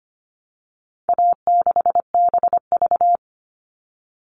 Morse code, heard as a keyed tone, about 25 wpm, 700 Hz. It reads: A6BV
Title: All Cut Numbers - 4-Digit 25wpm